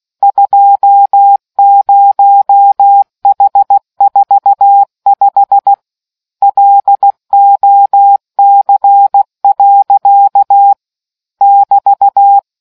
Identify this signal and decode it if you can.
Morse code, heard as a keyed tone, about 16 wpm, 800 Hz. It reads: 20H45 LOC. <BT>